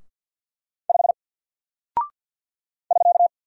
Morse code, heard as a keyed tone, about 48 wpm, 700 Hz. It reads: 5 <SK>
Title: Single Character (Letter:Number:Punctuation:Prosign) Alt 50wpm